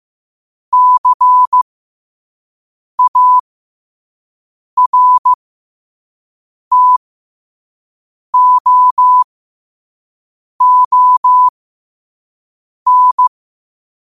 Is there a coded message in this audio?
Yes: Morse code from 0.7 to 13.3 s, heard as a keyed tone, about 15 wpm, spaced as for 5 wpm, 1000 Hz. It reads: CARTOON